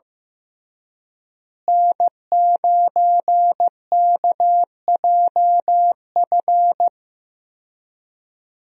Morse code, heard as a keyed tone, about 16 wpm, 700 Hz. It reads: N9KJF
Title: USA Call Signs - 5 Characters - Repeated 3x - 15wpm